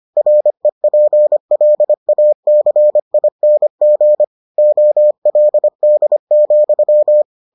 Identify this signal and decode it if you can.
Morse code, heard as a keyed tone, about 25 wpm, 600 Hz. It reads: REPLACING OLD,